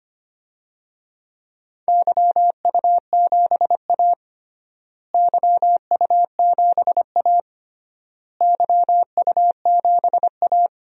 Morse code, heard as a keyed tone, about 25 wpm, 700 Hz. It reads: YU7A YU7A YU7A
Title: DXCC-100 Call Signs - 4 Characters - Repeated 3x - 25wpm